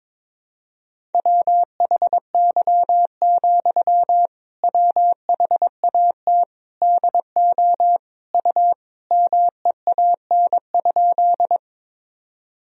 Morse code, heard as a keyed tone, about 21 wpm, 700 Hz. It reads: WHY, WHAT DO U MEAN?
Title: Sentences from Top 200 Words and Abbreviations 22wpm